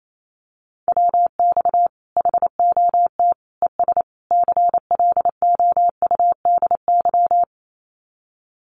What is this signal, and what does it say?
Morse code, heard as a keyed tone, about 28 wpm, 700 Hz. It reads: WX HOT ES CLOUDY
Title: Common QSO Phrases 28wpm